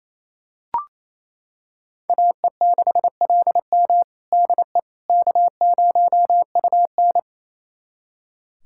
Morse code, heard as a keyed tone, about 28 wpm, 700 Hz. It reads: AE6LM DE K0UN
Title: USA Callsign DE Callsign 28wpm